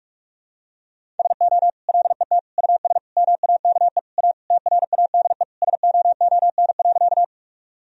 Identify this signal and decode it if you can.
Morse code, heard as a keyed tone, about 46 wpm, 700 Hz. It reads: SO LET US MAKE A TRADE SOON.
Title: Sentences from Top 1000 Words - Review Part 4 - 45wpm